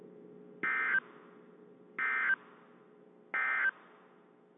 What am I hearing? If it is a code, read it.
Emergency Alert System (SAME) tones, heard as harsh buzzing bursts. It is heard from 0.5 to 3.7 s.
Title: WZ2523 - NOAA Weather Radio